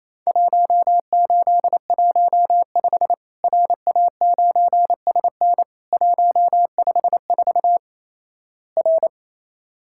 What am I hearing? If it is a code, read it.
Morse code, heard as a keyed tone, about 27 wpm, 700 Hz. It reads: 1815 RA9SN 154 R
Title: Worked All Europe DX Contest - QTC Exchange - 28wpm